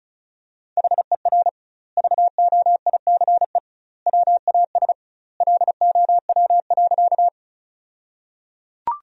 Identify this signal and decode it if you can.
Morse code, heard as a keyed tone, about 36 wpm, 700 Hz. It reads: HER VOICE WAS LOW.
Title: 2x Word-Spacing - Sentences from Top 500 Words - 35wpm